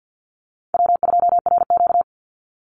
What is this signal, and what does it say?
Morse code, heard as a keyed tone, about 50 wpm, 700 Hz. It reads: W2RQ
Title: Call Signs Repeated 3x - 50wpm